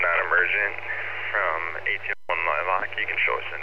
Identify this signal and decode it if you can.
Emergency Alert System (SAME) tones, heard as harsh buzzing bursts, with speech. It is heard from 0.2 to 3.3 s.